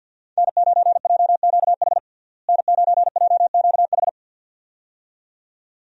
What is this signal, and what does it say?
Morse code, heard as a keyed tone, about 53 wpm, 700 Hz. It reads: N9JQH N9JQH
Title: USA Call Signs - 5 Characters - Repeated 2x - 50wpm